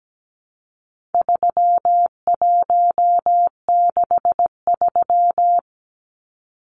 Morse code, heard as a keyed tone, about 17 wpm, 700 Hz. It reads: 3163
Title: Numbers - 4-Digit 17wpm